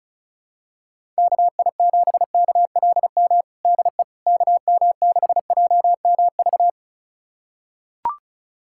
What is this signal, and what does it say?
Morse code, heard as a keyed tone, about 36 wpm, 700 Hz. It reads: KI7KLM DE KM6JMV
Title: USA Callsign DE Callsign - 6 Characters - 35wpm